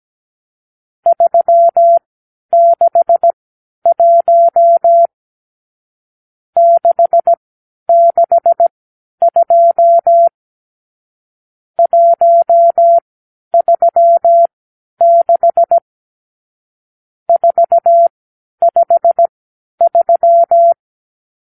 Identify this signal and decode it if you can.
Morse code, heard as a keyed tone, about 17 wpm, spaced as for 10 wpm, 680 Hz. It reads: 361 662 136 453
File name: k7qo / Lesson 50 (1-6)